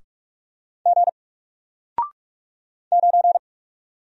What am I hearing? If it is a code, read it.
Morse code, heard as a keyed tone, about 44 wpm, 700 Hz. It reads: G 9
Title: Single Character (Letter:Number:Punctuation:Prosign) 45wpm